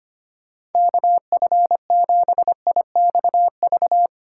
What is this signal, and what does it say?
Morse code, heard as a keyed tone, about 33 wpm, 700 Hz. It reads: KF7IXV